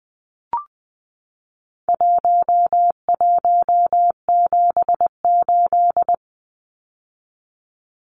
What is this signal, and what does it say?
Morse code, heard as a keyed tone, about 20 wpm, 700 Hz. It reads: 1178
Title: Numbers - 4-Digit 20wpm